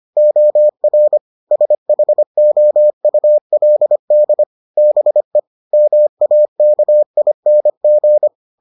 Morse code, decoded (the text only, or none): OR SHOULD BE MAKING